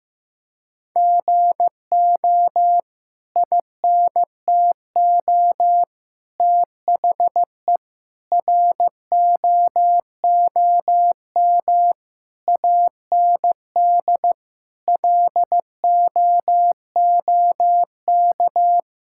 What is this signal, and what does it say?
Morse code, heard as a keyed tone, about 15 wpm, 700 Hz. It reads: GO INTO THE ROOM AND LOOK
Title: Sentences from Top 1000 Words - Review Part 6 - 15wpm